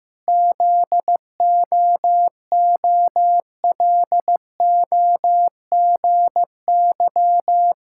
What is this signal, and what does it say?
Morse code, heard as a keyed tone, about 15 wpm, 700 Hz. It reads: ZOOLOGY
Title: Pangram Challenge v2 - 15wpm